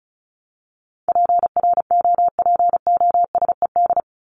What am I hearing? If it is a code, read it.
Morse code, heard as a keyed tone, about 34 wpm, 700 Hz. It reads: PROPOSED
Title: ED Suffix - Top 100 Words with Base Words 35wpm